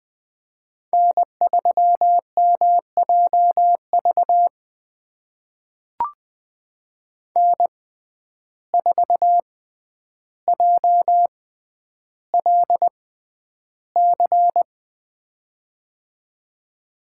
Morse code, heard as a keyed tone, about 20 wpm, spaced as for 10 wpm, 700 Hz. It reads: N3MJV N4JLC
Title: USA Call Signs - 5 Characters - 6x Character Spacing - 20wpm